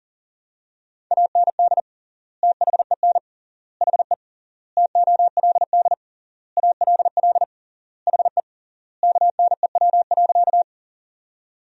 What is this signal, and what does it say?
Morse code, heard as a keyed tone, about 40 wpm, 700 Hz. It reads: AND THEN HE TOLD ALL HE KNEW.
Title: 3x Word-Spacing - Sentences from Top 400 Words - 40wpm